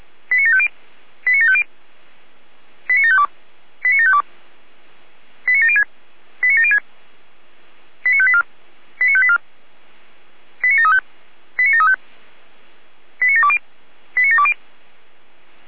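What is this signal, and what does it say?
Five-tone selective call (ZVEI), heard as quick runs of five tones, tones of about 70 ms. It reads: 89750 89750 89752 89752 89706 89706 89584 89584 89735 89735 89720 89720
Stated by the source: ZVEI five-tone call 89465 received on SDR.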